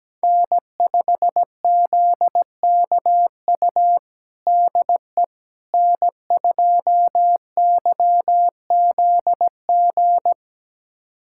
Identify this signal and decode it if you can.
Morse code, heard as a keyed tone, about 17 wpm, 700 Hz. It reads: N5ZKU DE N2YZG
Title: USA Callsign DE Callsign - 5 Characters - 17wpm